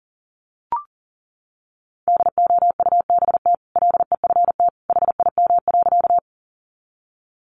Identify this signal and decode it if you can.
Morse code, heard as a keyed tone, about 40 wpm, 700 Hz. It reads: DOUBT LEFT HIM.
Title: Sentences from Top 1500 Words 40wpm